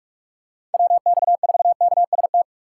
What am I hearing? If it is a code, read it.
Morse code, heard as a keyed tone, about 46 wpm, 700 Hz. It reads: WX4KST